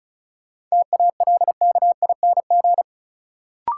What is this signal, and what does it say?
Morse code, heard as a keyed tone, about 36 wpm, 700 Hz. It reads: TALKING
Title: Sets of 2 Words - New Word Drill - 35wpm